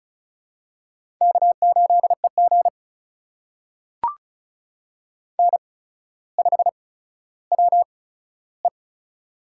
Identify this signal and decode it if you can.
Morse code, heard as a keyed tone, about 34 wpm, spaced as for 15 wpm, 700 Hz. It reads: K8EG N5WE
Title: USA Call Signs - 4 Characters - 8x Character Spacing - 35wpm